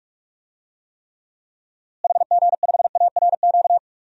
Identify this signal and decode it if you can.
Morse code, heard as a keyed tone, about 46 wpm, 700 Hz. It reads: HG5ARQ
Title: DXCC-100 Call Signs - 6 Characters - Repeated 2x - 45wpm